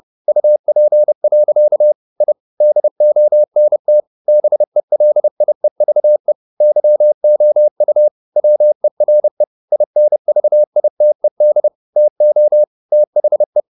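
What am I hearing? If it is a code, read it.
Morse code, heard as a keyed tone, about 30 wpm, 600 Hz. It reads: UP. I DONT BELIEVE YOU WERE INVITED TO THE
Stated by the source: Morse code at 30 wpm, the novel Never by Ken Follett in English, from CW med Gnister.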